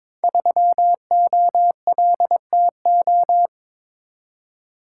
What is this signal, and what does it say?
Morse code, heard as a keyed tone, about 22 wpm, 700 Hz. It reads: 3OLTO
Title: Code Groups - 5 Letter:Numbers - 22wpm